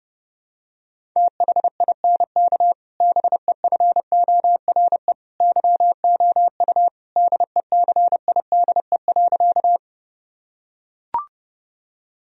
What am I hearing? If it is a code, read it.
Morse code, heard as a keyed tone, about 32 wpm, 700 Hz. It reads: THINK BEFORE YOU DECIDE.
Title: Sentences from Top 1000 Words - Review Part 3 - 30wpm